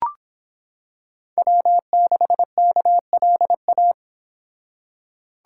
Morse code, heard as a keyed tone, about 26 wpm, 700 Hz. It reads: W6KLA